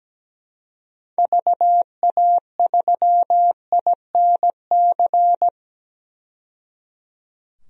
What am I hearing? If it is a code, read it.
Morse code, heard as a keyed tone, about 17 wpm, 700 Hz. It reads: VA3INC
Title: DXCC-100 Call Signs - 6 Characters - 17wpm